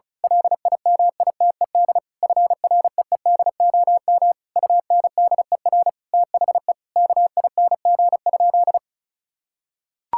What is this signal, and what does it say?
Morse code, heard as a keyed tone, about 36 wpm, 700 Hz. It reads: LIMITED FREEDOM UNDER THE KING?
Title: Sentences from Top 1000 Words - Review Part 6 - 35wpm